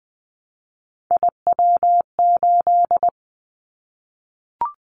Morse code, heard as a keyed tone, about 20 wpm, 700 Hz. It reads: IW8
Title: Callsign Prefixes for Top 25 DXCC - 20wpm